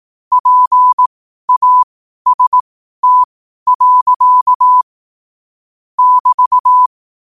Morse code, heard as a keyed tone, about 18 wpm, spaced as for 12 wpm, 1000 Hz. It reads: PAST. <BT>
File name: qotdfiles / 12wpm